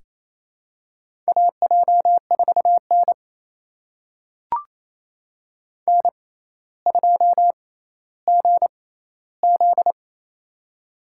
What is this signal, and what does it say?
Morse code, heard as a keyed tone, about 28 wpm, spaced as for 14 wpm, 700 Hz. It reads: AJ4N N2GZ
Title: USA Call Signs - 6x Character Spacing - 28wpm